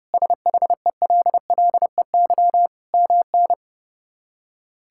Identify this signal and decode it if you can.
Morse code, heard as a keyed tone, about 30 wpm, 700 Hz. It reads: SHELLEY MN
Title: K1USN SST QSO Practice - 30wpm